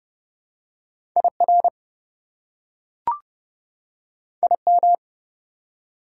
Morse code, heard as a keyed tone, about 30 wpm, 700 Hz. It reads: IR IM
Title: Prefixes and Suffixes 30wpm